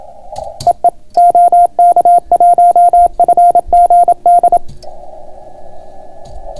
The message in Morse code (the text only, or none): EE OK1FGD